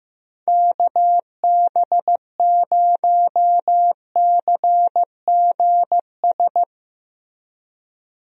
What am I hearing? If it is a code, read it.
Morse code, heard as a keyed tone, about 15 wpm, 700 Hz. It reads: KB0CGS
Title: USA Call Signs - 6 Characters - Repeated 2x - 15wpm